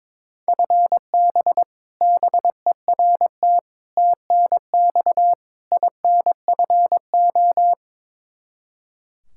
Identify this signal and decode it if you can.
Morse code, heard as a keyed tone, about 22 wpm, 700 Hz. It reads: FB BERT TNX INFO
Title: Practice QSO - Exchange 3 - More Info - 22wpm